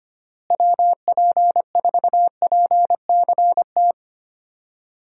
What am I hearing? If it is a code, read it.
Morse code, heard as a keyed tone, about 25 wpm, 700 Hz. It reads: WP4PCT
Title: USA Call Signs - 6 Characters - Repeated 3x - 25wpm